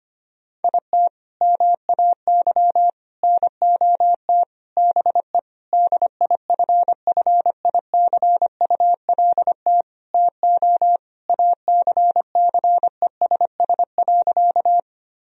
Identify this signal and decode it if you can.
Morse code, heard as a keyed tone, about 25 wpm, 700 Hz. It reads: IT MAY NOT BE DIFFICULT TO ACCESS.
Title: Sentences from Top 1000 Words - Review Part 4 - 25wpm